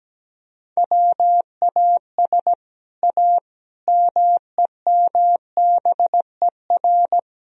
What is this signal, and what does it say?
Morse code, heard as a keyed tone, about 17 wpm, 700 Hz. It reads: WAS A MEMBER
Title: Speed-Racing - Sets of 5 Words 17wpm